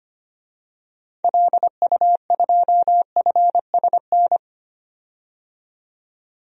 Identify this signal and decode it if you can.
Morse code, heard as a keyed tone, about 25 wpm, 700 Hz. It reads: LU2FSN
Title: DXCC-100 Call Signs - 6 Characters - 25wpm